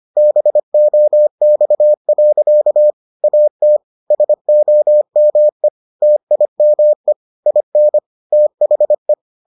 Morse code, heard as a keyed tone, about 25 wpm, 600 Hz. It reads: BOX. AT SOME TIME IN THE